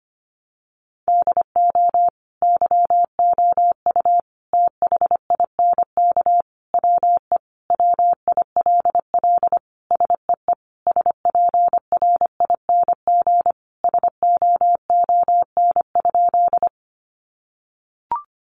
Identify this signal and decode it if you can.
Morse code, heard as a keyed tone, about 25 wpm, 700 Hz. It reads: DO YOU THINK WE WILL SEE SPRING SOON?
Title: Sentences from Top 1200 Words 25wpm